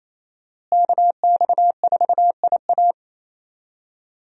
Morse code, heard as a keyed tone, about 27 wpm, 700 Hz. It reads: KX4IA
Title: USA Call Signs - 5 Characters - Repeated 3x - 28wpm